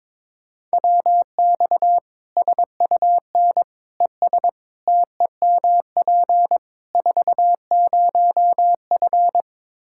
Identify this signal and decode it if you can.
Morse code, heard as a keyed tone, about 22 wpm, 700 Hz. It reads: WX SUN ES TEMP 40F
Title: Practice QSO - Exchange 2 - Info - 22wpm